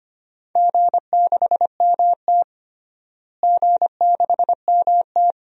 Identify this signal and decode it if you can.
Morse code, heard as a keyed tone, about 25 wpm, 700 Hz. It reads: G6MT G6MT
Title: DXCC-100 Call Signs - 4 Characters - Repeated 2x - 25wpm